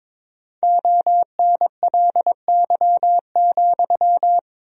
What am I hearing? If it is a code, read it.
Morse code, heard as a keyed tone, about 21 wpm, 700 Hz. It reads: ONLY,
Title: 4x Word-Spacing - Sentences from Top 300 Words - 22wpm